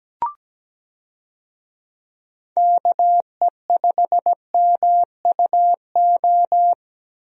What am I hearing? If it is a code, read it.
Morse code, heard as a keyed tone, about 17 wpm, 700 Hz. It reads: KE5MUO
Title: USA Call Signs - 6 Characters - 17wpm